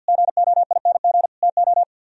Morse code, heard as a keyed tone, about 62 wpm, 700 Hz. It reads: GOING TO
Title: Speed-Racing - Sets of 4 Words 50wpm